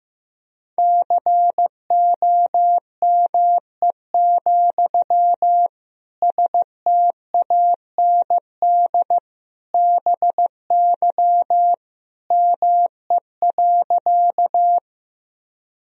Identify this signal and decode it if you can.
Morse code, heard as a keyed tone, about 16 wpm, 700 Hz. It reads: COME, STAND BY ME.
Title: Sentences from Top 1000 Words - Review Part 1 - 15wpm